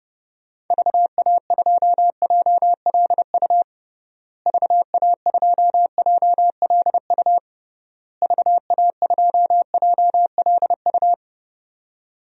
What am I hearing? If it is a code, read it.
Morse code, heard as a keyed tone, about 30 wpm, 700 Hz. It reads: VA2JLU VA2JLU VA2JLU
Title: DXCC-100 Call Signs - 6 Characters - Repeated 3x - 30wpm